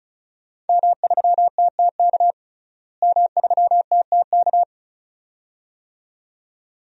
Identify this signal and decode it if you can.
Morse code, heard as a keyed tone, about 34 wpm, 700 Hz. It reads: M3TTK M3TTK
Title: DXCC-100 Call Signs - 5 Characters - Repeated 2x - 35wpm